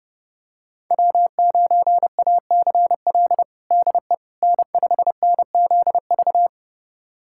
Morse code, heard as a keyed tone, about 30 wpm, 700 Hz. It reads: W9ACL DE N5NZV
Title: USA Callsign DE Callsign - 5 Characters - 30wpm